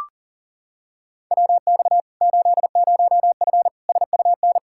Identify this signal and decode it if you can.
Morse code, heard as a keyed tone, about 40 wpm, 700 Hz. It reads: WX 80F SUN
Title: Practice QSO - Exchange 2 - Info - 40wpm